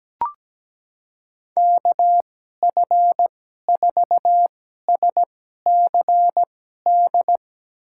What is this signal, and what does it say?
Morse code, heard as a keyed tone, about 17 wpm, spaced as for 12 wpm, 700 Hz. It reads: KF4SCD
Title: USA Call Signs - 6 Characters - 2x Character Spacing - 17wpm